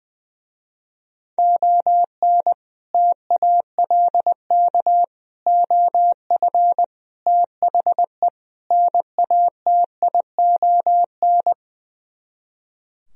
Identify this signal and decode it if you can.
Morse code, heard as a keyed tone, about 21 wpm, 700 Hz. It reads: ON TALK OF THE NATION
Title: Speed-Racing - Sets of 5 Words 20wpm